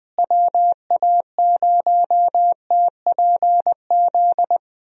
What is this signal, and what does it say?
Morse code, heard as a keyed tone, about 20 wpm, 700 Hz. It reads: WA0TPZ
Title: USA Call Signs - 6 Characters - Repeated 2x - 20wpm